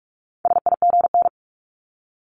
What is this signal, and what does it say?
Morse code, heard as a keyed tone, about 46 wpm, 700 Hz. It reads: SIGN
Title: Multi-letter Phonemes Repeated 3x 45wpm